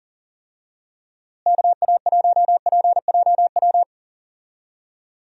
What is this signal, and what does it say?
Morse code, heard as a keyed tone, about 40 wpm, 700 Hz. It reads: KA1PJW
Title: USA Call Signs - 6 Characters - Repeated 3x - 40wpm